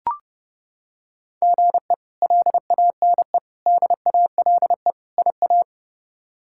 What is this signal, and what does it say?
Morse code, heard as a keyed tone, about 32 wpm, 700 Hz. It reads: GE LANE DALE IA
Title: K1USN SST QSO Practice - 30wpm